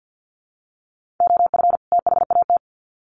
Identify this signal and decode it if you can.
Morse code, heard as a keyed tone, about 50 wpm, 700 Hz. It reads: OF THAT
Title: Sets of 2 Words 50wpm